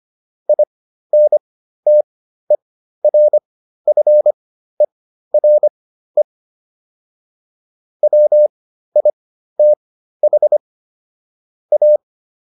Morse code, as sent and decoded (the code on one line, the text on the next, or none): .. -. - . .-. ..-. .-. / .-- .. - .... / .-
INTERFERE WITH A